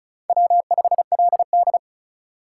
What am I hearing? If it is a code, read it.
Morse code, heard as a keyed tone, about 36 wpm, 700 Hz. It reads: W5LD
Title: USA Call Signs - 4 Characters - 2x Character Spacing - 35wpm